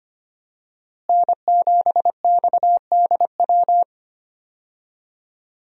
Morse code, heard as a keyed tone, about 25 wpm, 700 Hz. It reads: N7XDW
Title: USA Call Signs - 5 Characters - Repeated 3x - 25wpm